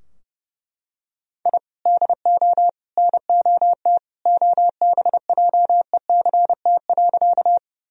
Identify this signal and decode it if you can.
Morse code, heard as a keyed tone, about 32 wpm, 700 Hz. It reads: I DO NOT OBJECT.
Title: Sentences from Top 1000 Words - Review Part 3 - 30wpm